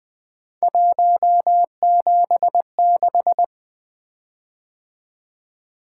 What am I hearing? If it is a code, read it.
Morse code, heard as a keyed tone, about 20 wpm, 700 Hz. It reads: 176
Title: Three Letter-Number Combinations - 20wpm